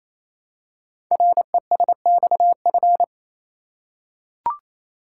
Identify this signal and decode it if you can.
Morse code, heard as a keyed tone, about 27 wpm, 700 Hz. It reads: RESXF
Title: Code Groups - 5 Letters - 28wpm